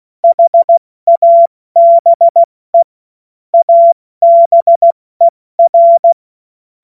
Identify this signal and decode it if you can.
Morse code, heard as a keyed tone, about 16 wpm, 680 Hz. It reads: HABE ABER
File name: Morsekurs / Witze / Witz 16. Februar 2021, 70 F80